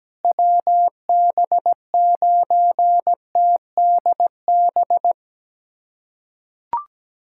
Morse code, heard as a keyed tone, about 17 wpm, 700 Hz. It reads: WB9TDB
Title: USA Callsign DE Callsign 17wpm